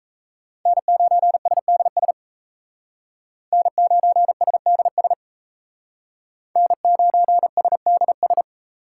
Morse code, heard as a keyed tone, about 36 wpm, 700 Hz. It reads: N9SDS N9SDS N9SDS